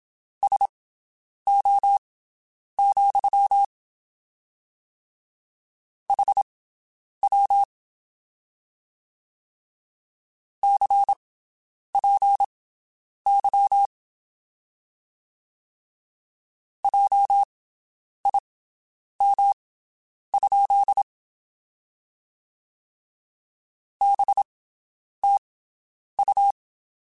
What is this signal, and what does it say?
Morse code, heard as a keyed tone, about 27 wpm, spaced as for 9 wpm, 800 Hz. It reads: SO, HW CPY JIM? BTU
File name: (spaced) / QSOs Spaced-mp3s / WD0BGP 1 @25 sp